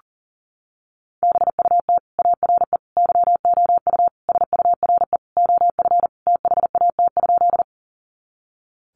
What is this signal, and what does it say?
Morse code, heard as a keyed tone, about 40 wpm, 700 Hz. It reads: BUT ARE YOU SURE OF THAT?
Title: Sentences from Top 300 Words 40wpm